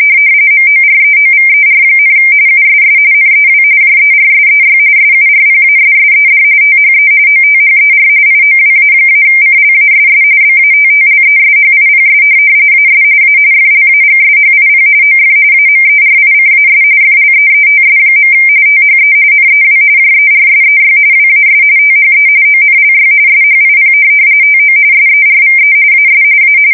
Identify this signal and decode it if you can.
RTTY, about 45.45 baud. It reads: MOREOVER, USS NEW JERSEY SURPASSED THE OTHER IOWA-CLASS BATTLESHIPS AS A FLAGSHIP OF AMERICAS FLEETS. THE NEW JERSEY WAS THE ONLY SHIP TO SERVE AS A
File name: fsk170 / USS-New-Jersey